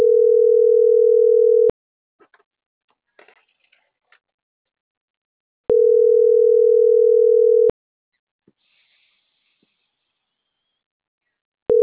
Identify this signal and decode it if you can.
Ringback tone (North America), 440 + 480 Hz.